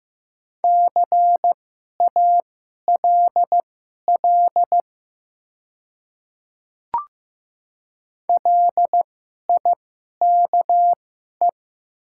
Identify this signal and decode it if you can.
Morse code, heard as a keyed tone, about 15 wpm, spaced as for 11 wpm, 700 Hz. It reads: CALL LIKE
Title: 2x Character Spacing - Top 100 Words - 15wpm